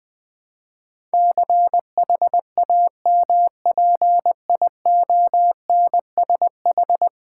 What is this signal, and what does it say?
Morse code, heard as a keyed tone, about 20 wpm, 700 Hz. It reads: CHAMPIONSH
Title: Build A Word - 5 Letters At A Time - 12-Letter Words - 20wpm